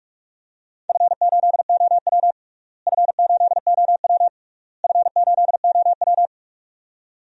Morse code, heard as a keyed tone, about 46 wpm, 700 Hz. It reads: F8OW F8OW F8OW
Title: DXCC-100 Call Signs - 4 Characters - Repeated 3x - 45wpm